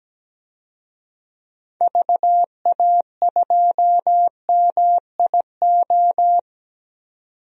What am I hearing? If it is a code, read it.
Morse code, heard as a keyed tone, about 17 wpm, 700 Hz. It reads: VA2MIO
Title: DXCC-100 Call Signs - 6 Characters - 17wpm